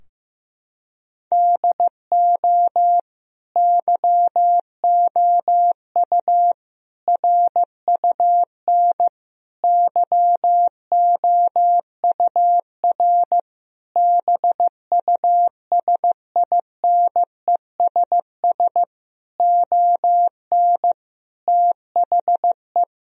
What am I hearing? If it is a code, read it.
Morse code, heard as a keyed tone, about 16 wpm, 700 Hz. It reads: DO YOU RUN YOUR BUSINESS ON THE
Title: Sentences from Top 1000 Words - Review Part 4 - 15wpm